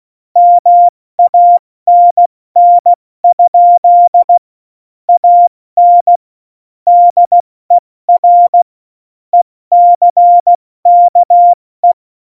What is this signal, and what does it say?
Morse code, heard as a keyed tone, about 16 wpm, 700 Hz. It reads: MANN? AN DER ECKE
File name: Morsekurs / Witze / Witz 26. September 17, Tempo 70